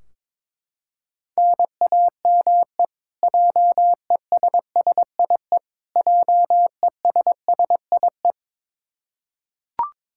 Morse code, heard as a keyed tone, about 22 wpm, 700 Hz. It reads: NAME JESSIE JESSIE
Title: Practice QSO - Exchange 1 - Beginning - 22wpm